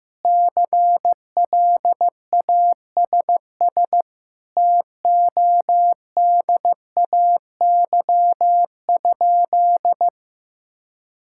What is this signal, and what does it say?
Morse code, heard as a keyed tone, about 15 wpm, 700 Hz. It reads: CLASS TODAY?
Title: Sentences from Top 1000 Words - Review Part 6 - 15wpm